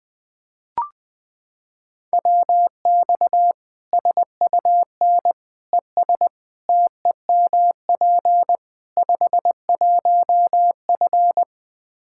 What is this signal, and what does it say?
Morse code, heard as a keyed tone, about 20 wpm, 700 Hz. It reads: WX SUN ES TEMP 51F
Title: Practice QSO - Exchange 2 - Info - 20wpm